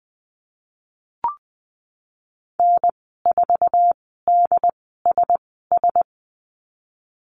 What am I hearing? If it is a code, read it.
Morse code, heard as a keyed tone, about 20 wpm, spaced as for 14 wpm, 700 Hz. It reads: N4DSS
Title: 069 Lesson 12g - Letter d - Call Signs 20wpm